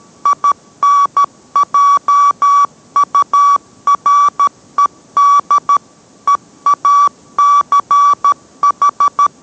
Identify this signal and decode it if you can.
Morse code, heard as a keyed tone, about 13 wpm, 1150 Hz. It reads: INJURED EACH